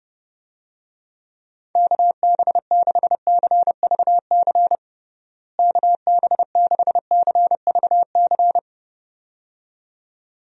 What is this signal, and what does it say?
Morse code, heard as a keyed tone, about 30 wpm, 700 Hz. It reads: KB6CVC KB6CVC
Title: USA Call Signs - 6 Characters - Repeated 2x - 30wpm